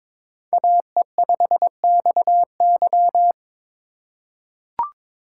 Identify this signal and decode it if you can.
Morse code, heard as a keyed tone, about 22 wpm, 700 Hz. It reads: AE5XY
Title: USA Call Signs - 5 Characters - 8x Character Spacing - 22wpm